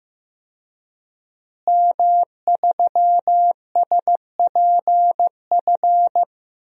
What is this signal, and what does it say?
Morse code, heard as a keyed tone, about 15 wpm, 700 Hz. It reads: M3SPF
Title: DXCC-100 Call Signs - 5 Characters - Repeated 3x - 15wpm